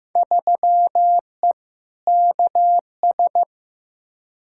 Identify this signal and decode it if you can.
Morse code, heard as a keyed tone, about 15 wpm, 700 Hz. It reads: 3E KS
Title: Field Day Exchanges Encore - 15wpm